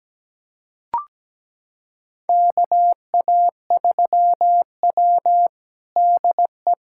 Morse code, heard as a keyed tone, about 17 wpm, 700 Hz. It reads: KA3W DE
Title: USA Callsign DE Callsign - 4 Characters - 17wpm